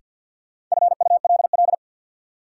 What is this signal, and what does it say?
Morse code, heard as a keyed tone, about 47 wpm, 700 Hz. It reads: FULL
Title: Sentences from Top 500 Words - New Word Drill - 50wpm